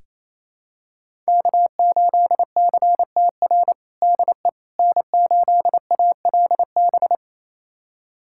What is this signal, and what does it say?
Morse code, heard as a keyed tone, about 28 wpm, 700 Hz. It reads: K8CTR DE N8ALB
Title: USA Callsign DE Callsign - 5 Characters - 28wpm